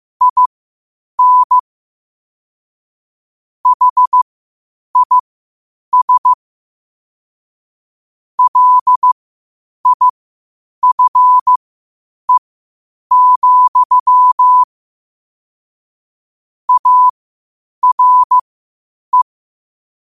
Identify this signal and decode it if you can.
Morse code, heard as a keyed tone, about 15 wpm, spaced as for 8 wpm, 1000 Hz. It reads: IN HIS LIFE, ARE
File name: qotdfiles / 8wpm